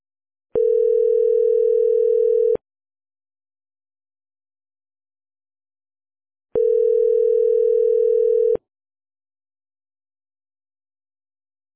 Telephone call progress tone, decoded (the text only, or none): ringback tone